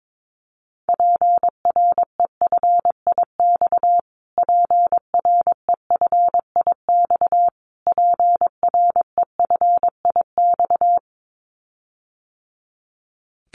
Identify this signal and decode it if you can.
Morse code, heard as a keyed tone, about 22 wpm, 700 Hz. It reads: PREFIX PREFIX PREFIX
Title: Callsign Prefixes for Top 25 DXCC - Repeated 3x - 22wpm